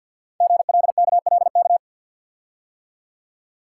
Morse code, heard as a keyed tone, about 51 wpm, 700 Hz. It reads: GLKLK
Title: Code Groups - 5 Letters - 50wpm